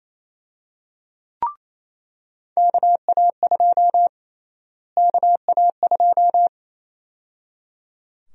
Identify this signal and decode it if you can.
Morse code, heard as a keyed tone, about 28 wpm, 700 Hz. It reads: KA2 KA2
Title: USA Call Sign Prefixes - Repeated 2x - 28wpm